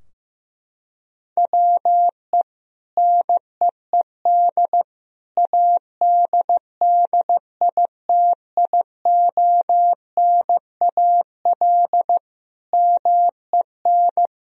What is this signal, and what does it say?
Morse code, heard as a keyed tone, about 15 wpm, 700 Hz. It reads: WE NEED ADDITIONAL MEN
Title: Sentences from Top 1000 Words - Review Part 4 - 15wpm